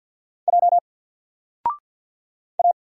Morse code, heard as a keyed tone, about 48 wpm, 700 Hz. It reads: J A
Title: Single Letter-Number 50wpm